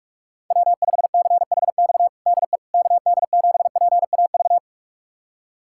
Morse code, heard as a keyed tone, about 42 wpm, 700 Hz. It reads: W5CHX DE KD7PAV
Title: USA Callsign DE Callsign 45wpm